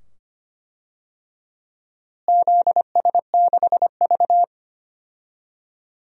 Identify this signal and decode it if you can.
Morse code, heard as a keyed tone, about 25 wpm, 700 Hz. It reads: ZS6V
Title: DXCC-100 Call Signs - 4 Characters - Repeated 2x - 25wpm